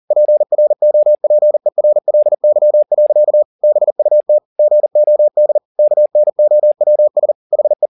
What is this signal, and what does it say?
Morse code, heard as a keyed tone, about 40 wpm, 600 Hz. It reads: PROPERLY. BUT GOD KNOWS HE
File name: the old man and the sea 2 40wpm